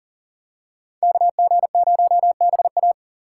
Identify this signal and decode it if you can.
Morse code, heard as a keyed tone, about 38 wpm, 700 Hz. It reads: KG0BA
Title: USA Call Signs - 2x Character Spacing - 40wpm